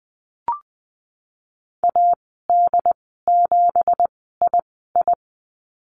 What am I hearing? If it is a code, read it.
Morse code, heard as a keyed tone, about 20 wpm, spaced as for 15 wpm, 700 Hz. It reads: AD7II
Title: 186 Lesson 30g - Letter b - Call Signs 20wpm